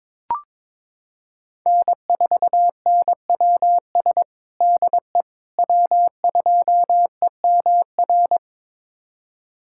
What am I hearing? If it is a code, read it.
Morse code, heard as a keyed tone, about 22 wpm, 700 Hz. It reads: N4NWS DE W2EMR
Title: USA Callsign DE Callsign - 5 Characters - 22wpm